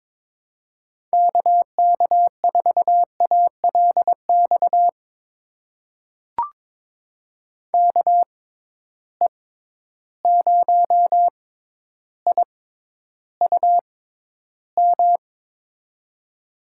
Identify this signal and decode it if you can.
Morse code, heard as a keyed tone, about 22 wpm, spaced as for 11 wpm, 700 Hz. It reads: KK4ALX KE0IUM
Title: USA Call Signs - 6 Characters - 6x Character Spacing - 22wpm